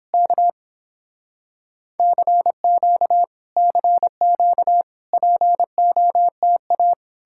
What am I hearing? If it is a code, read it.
Morse code, heard as a keyed tone, about 31 wpm, 700 Hz. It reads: K CQ CQ POTA